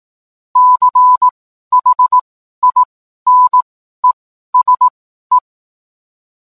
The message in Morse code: CHINESE

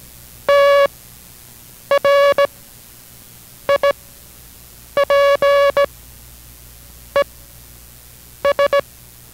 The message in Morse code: TRIPES

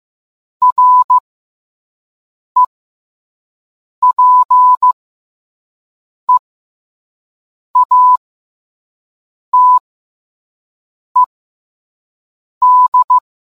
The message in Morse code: REPEATED

